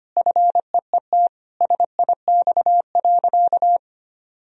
Morse code, heard as a keyed tone, about 25 wpm, 700 Hz. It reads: FEET SIX.
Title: Sentences from Top 1000 Words - Review Part 2 - 25wpm